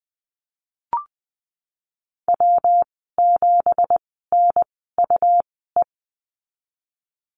Morse code, heard as a keyed tone, about 20 wpm, spaced as for 15 wpm, 700 Hz. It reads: W7NUE
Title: 186 Lesson 30g - Letter b - Call Signs 20wpm